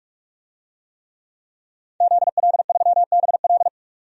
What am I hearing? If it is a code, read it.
Morse code, heard as a keyed tone, about 43 wpm, 700 Hz. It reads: ZL3BL
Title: DXCC-100 Call Signs - 5 Characters - 45wpm